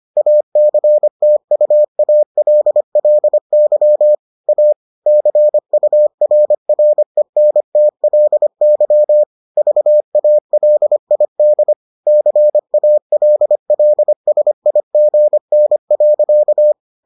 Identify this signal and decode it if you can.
Morse code, heard as a keyed tone, about 25 wpm, 600 Hz. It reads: ACTUALLY A CURRENTLY VALID CALLSIGN.